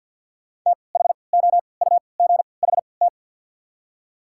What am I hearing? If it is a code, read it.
Morse code, heard as a keyed tone, about 53 wpm, spaced as for 29 wpm, 690 Hz. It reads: THOUGHT
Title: 3x Character Spacing - Top 300 Words - 50wpm